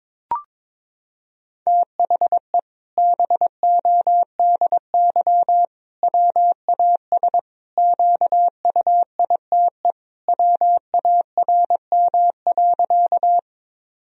Morse code, heard as a keyed tone, about 22 wpm, 700 Hz. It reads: THE BODY WAS QUITE WARM.
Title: Sentences from Top 1000 Words - Review Part 2 - 22wpm